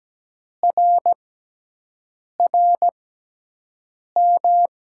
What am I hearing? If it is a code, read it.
Morse code, heard as a keyed tone, about 17 wpm, spaced as for 6 wpm, 700 Hz. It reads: RRM
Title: USA Call Signs - 6 Characters - 6x Character Spacing - 17wpm